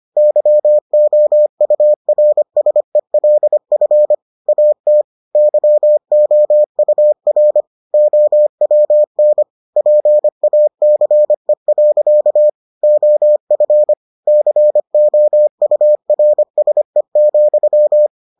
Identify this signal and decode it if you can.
Morse code, heard as a keyed tone, about 25 wpm, 600 Hz. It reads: YOURSELF AT YOUR OWN PACE. OF COURSE,